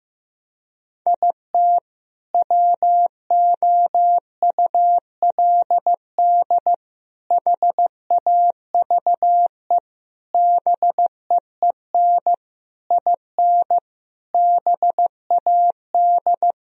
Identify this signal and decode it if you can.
Morse code, heard as a keyed tone, about 15 wpm, 700 Hz. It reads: IT WOULD HAVE BEEN IN BAD
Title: Sentences from Top 1000 Words - Review Part 1 - 15wpm